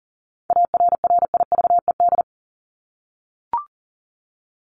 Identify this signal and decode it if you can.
Morse code, heard as a keyed tone, about 40 wpm, 700 Hz. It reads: ARRIVED
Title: ED Suffix - Top 100 Words with Base Words 40wpm